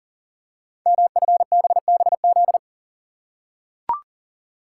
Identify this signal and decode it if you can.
Morse code, heard as a keyed tone, about 40 wpm, 690 Hz. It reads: MFBBZ
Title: Code Groups - 5 Letters - 40wpm